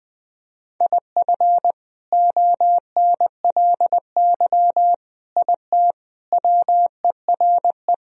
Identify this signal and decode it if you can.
Morse code, heard as a keyed tone, about 21 wpm, 700 Hz. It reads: IF ONLY IT WERE
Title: Sentences from Top 500 Words and Abbreviations 20wpm